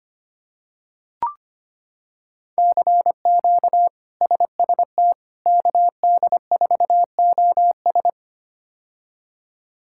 Morse code, heard as a keyed tone, about 25 wpm, 700 Hz. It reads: CQ SST KD4OS
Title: K1USN SST QSO Practice - 25wpm